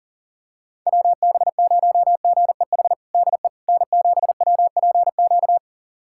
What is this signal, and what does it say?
Morse code, heard as a keyed tone, about 40 wpm, 700 Hz. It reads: WB0GEH DE N7WPQ
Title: USA Callsign DE Callsign 40wpm